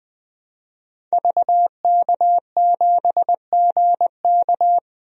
Morse code, heard as a keyed tone, about 20 wpm, 700 Hz. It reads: VK7GK
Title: DXCC-100 Call Signs - 5 Characters - 20wpm